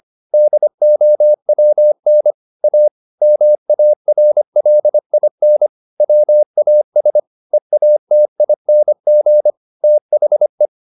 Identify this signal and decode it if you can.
Morse code, heard as a keyed tone, about 25 wpm, 600 Hz. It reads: DOWN A MARLIN WAS EATING THE
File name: the old man and the sea part 1 25wpm